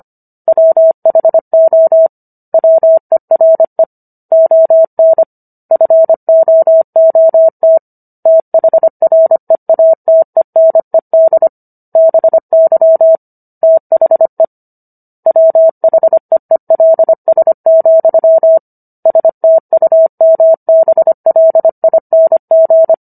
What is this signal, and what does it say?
Morse code, heard as a keyed tone, about 25 wpm, 650 Hz. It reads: WHO WERE ON FOOT THREATENED BY THE WHEELS, STUMBLING